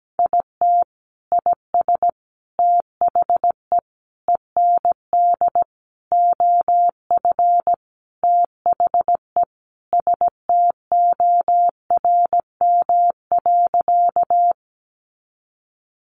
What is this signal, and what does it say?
Morse code, heard as a keyed tone, about 17 wpm, 700 Hz. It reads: IT IS THE END OF THE STORM.
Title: Sentences from Top 1500 Words 17wpm